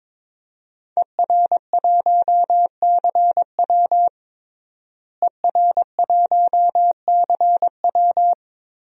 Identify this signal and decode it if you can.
Morse code, heard as a keyed tone, about 22 wpm, 700 Hz. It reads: ER1CW ER1CW
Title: DXCC-100 Call Signs - 5 Characters - Repeated 3x - 22wpm